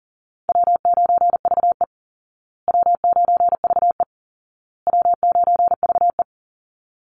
Morse code, heard as a keyed tone, about 40 wpm, 700 Hz. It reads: W9VE W9VE W9VE
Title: Call Signs Repeated 2x - 40wpm